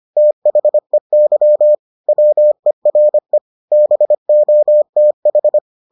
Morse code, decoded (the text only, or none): THEY WERE BOTH